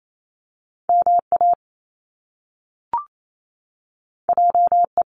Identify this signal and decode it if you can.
Morse code, heard as a keyed tone, about 28 wpm, 700 Hz. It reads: MA JE
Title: Top Two Letter Combinations in Words 28wpm